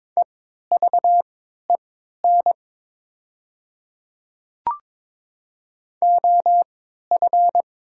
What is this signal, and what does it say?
Morse code, heard as a keyed tone, about 22 wpm, spaced as for 13 wpm, 700 Hz. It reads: EVEN OF
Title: 3x Character Spacing - Top 300 Words - 22wpm